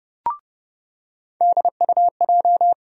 Morse code, heard as a keyed tone, about 30 wpm, 700 Hz. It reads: DUJ
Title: USA Call Sign Postfixes - 3 Letters 30wpm